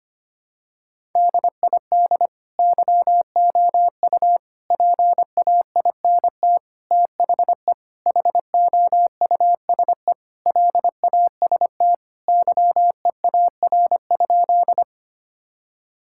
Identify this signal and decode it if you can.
Morse code, heard as a keyed tone, about 25 wpm, 700 Hz. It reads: DID YOU PAINT THE HOUSE LAST YEAR?
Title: Sentences from Top 1000 Words - Review Part 6 - 25wpm